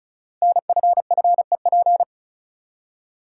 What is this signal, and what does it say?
Morse code, heard as a keyed tone, about 36 wpm, 700 Hz. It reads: NFFEP
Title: Code Groups - 5 Letters - 35wpm